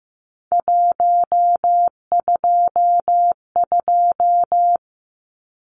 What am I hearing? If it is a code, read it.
Morse code, heard as a keyed tone, about 15 wpm, 700 Hz. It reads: 122
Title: Numbers - 3-Digit 15wpm